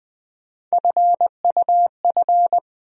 Morse code, heard as a keyed tone, about 20 wpm, 700 Hz. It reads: FUF
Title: USA Call Sign Postfixes - 3 Letters 20wpm